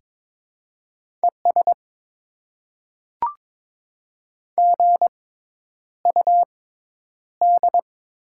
Morse code, heard as a keyed tone, about 22 wpm, spaced as for 10 wpm, 700 Hz. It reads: ES GUD
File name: Top 20 QSO Elements - 6x Character Spacing - 22wpm